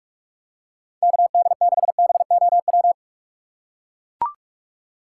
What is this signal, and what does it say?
Morse code, heard as a keyed tone, about 44 wpm, 700 Hz. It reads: KD6BOW
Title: USA Call Signs - 6 Characters - 6x Character Spacing - 45wpm